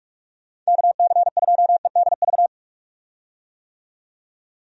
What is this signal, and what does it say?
Morse code, heard as a keyed tone, about 46 wpm, 700 Hz. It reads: KC2EDV
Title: USA Call Signs - 6 Characters - 45wpm